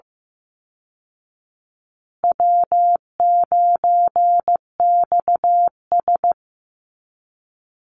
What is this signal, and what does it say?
Morse code, heard as a keyed tone, about 15 wpm, 700 Hz. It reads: W9XS
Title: Call Signs Repeated 2x - 15wpm